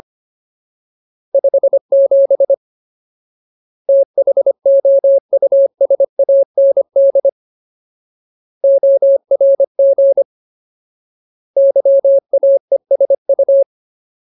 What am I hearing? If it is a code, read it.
Morse code, heard as a keyed tone, about 25 wpm, 550 Hz. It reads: 57 THOUSAND ORG YAESU
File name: Ultimate Word List / 25WPM Ultimate